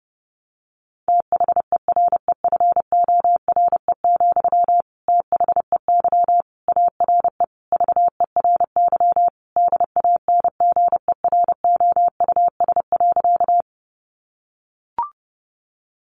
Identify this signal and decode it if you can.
Morse code, heard as a keyed tone, about 30 wpm, 700 Hz. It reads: THEREFORE, THEY ARE VERY DANGEROUS.
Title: Sentences from Top 1200 Words 30wpm